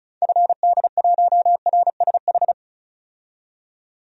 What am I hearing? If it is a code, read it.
Morse code, heard as a keyed tone, about 34 wpm, 700 Hz. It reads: FD1RSH
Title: DXCC-100 Call Signs - 6 Characters - 35wpm